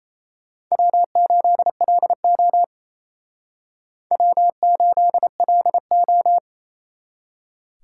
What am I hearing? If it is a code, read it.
Morse code, heard as a keyed tone, about 36 wpm, 700 Hz. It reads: W8LO W8LO